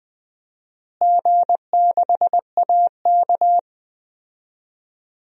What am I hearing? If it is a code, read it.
Morse code, heard as a keyed tone, about 20 wpm, 700 Hz. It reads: G6AK
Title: DXCC-100 Call Signs - 4 Characters - Repeated 3x - 20wpm